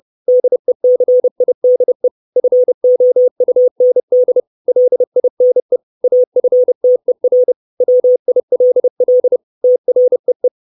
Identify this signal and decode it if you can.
Morse code, heard as a keyed tone, about 30 wpm, 500 Hz. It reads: DECIDE FOUND LINE AFTER WILL TREE
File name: 632 words / 632 Words 30WPM